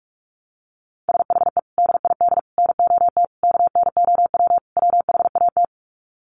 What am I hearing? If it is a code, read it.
Morse code, heard as a keyed tone, about 44 wpm, 700 Hz. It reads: SHE DID NOT KNOW WHAT
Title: Sets of 5 Words Encore2 45wpm